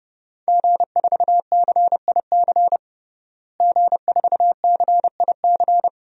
Morse code, heard as a keyed tone, about 30 wpm, 700 Hz. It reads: G4CIC G4CIC
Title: DXCC-100 Call Signs - 5 Characters - Repeated 3x - 30wpm